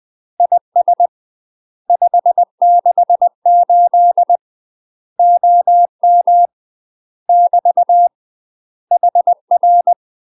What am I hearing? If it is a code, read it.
Morse code, heard as a keyed tone, about 20 wpm, 700 Hz. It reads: IS 568 OM <BT> HR
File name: UK MASTER QSOs TRAINING / 20WPM / UK MASTER QSOs 0073